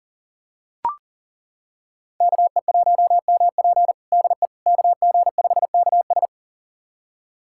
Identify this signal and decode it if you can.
Morse code, heard as a keyed tone, about 37 wpm, 700 Hz. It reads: KE1MP DE KG5KS
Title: USA Callsign DE Callsign - 5 Characters - 40wpm